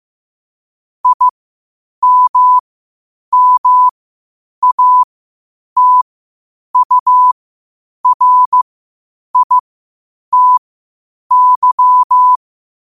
Morse code, heard as a keyed tone, about 15 wpm, spaced as for 8 wpm, 1000 Hz. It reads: IMMATURITY
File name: qotdfiles / 8wpm